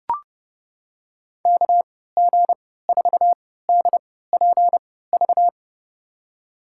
Morse code, heard as a keyed tone, about 32 wpm, spaced as for 17 wpm, 700 Hz. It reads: KG4DPV
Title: USA Call Signs - 6 Characters - 3x Character Spacing - 30wpm